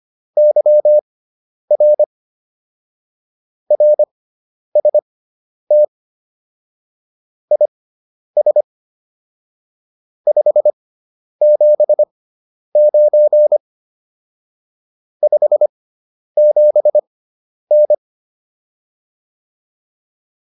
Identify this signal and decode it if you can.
Morse code, heard as a keyed tone, about 25 wpm, spaced as for 10 wpm, 600 Hz. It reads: YR RST IS 579 57N
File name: QSO304 10